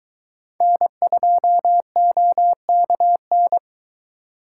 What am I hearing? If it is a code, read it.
Morse code, heard as a keyed tone, about 23 wpm, 700 Hz. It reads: N2OKN